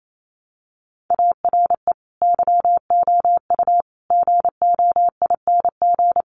Morse code, heard as a keyed tone, about 28 wpm, 700 Hz. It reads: ARE YOU GOING
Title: Sets of 5 Words 28wpm